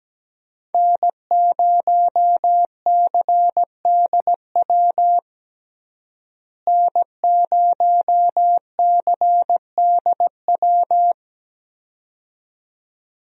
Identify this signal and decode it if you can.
Morse code, heard as a keyed tone, about 17 wpm, 700 Hz. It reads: N0CDW N0CDW
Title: USA Call Signs - 5 Characters - Repeated 2x - 17wpm